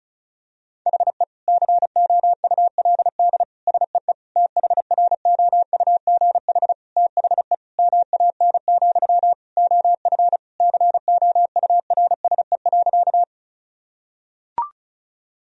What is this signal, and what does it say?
Morse code, heard as a keyed tone, about 34 wpm, 700 Hz. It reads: HE COULD SEE THROUGH THE MAN, OF COURSE.
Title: Sentences from Top 1000 Words - Review Part 1 - 35wpm